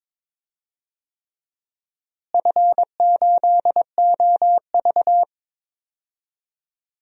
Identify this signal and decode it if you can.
Morse code, heard as a keyed tone, about 22 wpm, 700 Hz. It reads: F8OV
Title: DXCC-100 Call Signs - 4 Characters - Repeated 3x - 22wpm